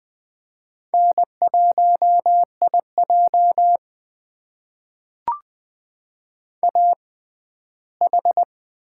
Morse code, heard as a keyed tone, about 21 wpm, spaced as for 14 wpm, 700 Hz. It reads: N1IJ AH
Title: USA Call Signs - 4 Characters - 6x Character Spacing - 20wpm